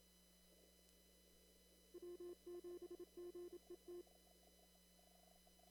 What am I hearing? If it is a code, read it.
Morse code, heard as a keyed tone, about 27 wpm, 350 Hz. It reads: W7GET
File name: CW / kt5j wpx cw-op-k5pi